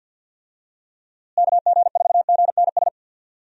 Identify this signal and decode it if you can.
Morse code, heard as a keyed tone, about 51 wpm, 700 Hz. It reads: KG4GNS
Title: USA Call Signs - 6 Characters - Repeated 2x - 50wpm